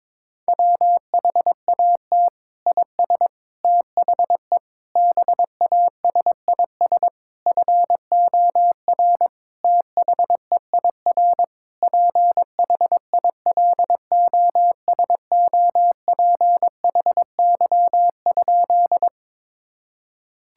Morse code, heard as a keyed tone, about 22 wpm, 700 Hz. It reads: WHAT IS THE BASIS FOR THEIR PHILOSOPHY?
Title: Sentences from Top 1000 Words - Review Part 6 - 22wpm